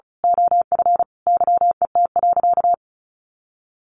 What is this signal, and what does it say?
Morse code, heard as a keyed tone, about 34 wpm, 700 Hz. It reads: OF YET.
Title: A Sound Of Thunder 35wpm